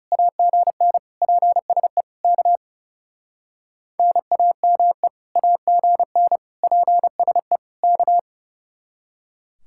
Morse code, heard as a keyed tone, about 39 wpm, 700 Hz. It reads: AGN PSE K NAME AGN PSE K